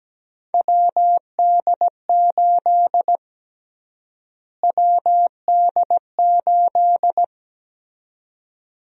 Morse code, heard as a keyed tone, about 17 wpm, 700 Hz. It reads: WD8 WD8
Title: USA Call Sign Prefixes - Repeated 2x - 17wpm